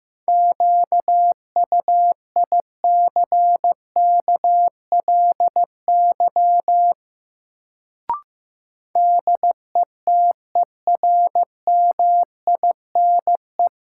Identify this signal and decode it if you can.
Morse code, heard as a keyed tone, about 15 wpm, 700 Hz. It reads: QUICKLY DETERMINE
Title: Sentences from Top 700 Words - New Word Drill - 15wpm